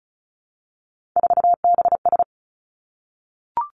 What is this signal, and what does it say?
Morse code, heard as a keyed tone, about 36 wpm, 700 Hz. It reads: 4BS
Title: Three Challenging Letter-Number Combinations 35wpm